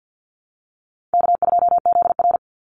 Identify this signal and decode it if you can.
Morse code, heard as a keyed tone, about 50 wpm, 700 Hz. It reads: K2ZR
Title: Call Signs - 50wpm